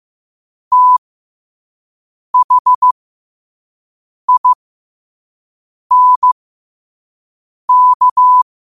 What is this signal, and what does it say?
Morse code, heard as a keyed tone, about 15 wpm, spaced as for 5 wpm, 1000 Hz. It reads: THINK
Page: New quotes every day in morse code at 5 Words per minute.